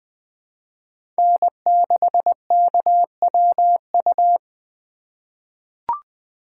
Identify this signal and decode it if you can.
Morse code, heard as a keyed tone, about 20 wpm, 700 Hz. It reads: N6KWU
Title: USA Call Signs - 2x Character Spacing - 20wpm